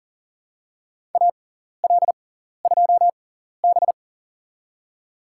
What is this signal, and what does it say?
Morse code, heard as a keyed tone, about 40 wpm, spaced as for 14 wpm, 700 Hz. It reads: AL2B
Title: USA Call Signs - 4 Characters - 6x Character Spacing - 40wpm